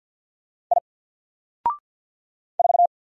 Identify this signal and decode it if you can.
Morse code, heard as a keyed tone, about 50 wpm, 700 Hz. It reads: I 4
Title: Single Letter-Number 50wpm